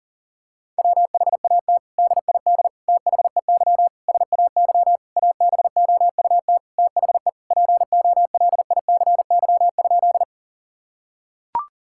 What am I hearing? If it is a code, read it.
Morse code, heard as a keyed tone, about 40 wpm, 700 Hz. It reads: WHAT DID THEY SAY ABOUT THE POLICY?
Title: Sentences from Top 1000 Words - Review Part 5 - 40wpm